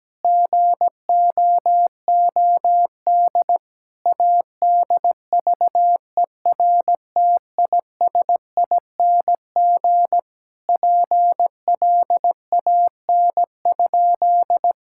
Morse code, heard as a keyed tone, about 17 wpm, 700 Hz. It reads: GOOD ADVERTISING PLAN?
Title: Sentences from Top 1000 Words - Review Part 6 - 17wpm